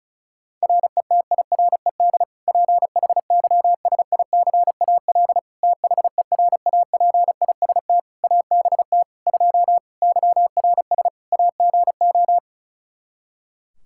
Morse code, heard as a keyed tone, about 34 wpm, 700 Hz. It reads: RETIRED PHYSICAL THERAPIST ABT 2 YRS AGO
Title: K1USN SST QSO Practice 35wpm